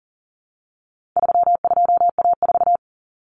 Morse code, heard as a keyed tone, about 40 wpm, 700 Hz. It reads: 32A4
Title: Common Cut Numbers - 4-Digit 40wpm